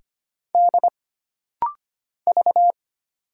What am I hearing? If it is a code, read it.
Morse code, heard as a keyed tone, about 25 wpm, 700 Hz. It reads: D V
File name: Single Character (Letter:Number:Punctuation:Prosign) - Rapid-Fire - 25wpm